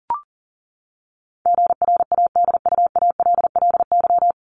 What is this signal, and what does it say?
Morse code, heard as a keyed tone, about 40 wpm, 700 Hz. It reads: GRADUALLY
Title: LY Suffix - Top 100 Words with Base Words 40wpm